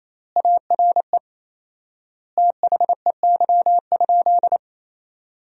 Morse code, heard as a keyed tone, about 28 wpm, 700 Hz. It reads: ARE THEY?
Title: 4x Word-Spacing - Sentences from Top 200 Words - 28wpm